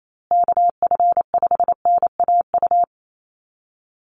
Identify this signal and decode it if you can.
Morse code, heard as a keyed tone, about 28 wpm, 700 Hz. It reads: KF5NAU
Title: USA Call Signs Repeated 2x 28wpm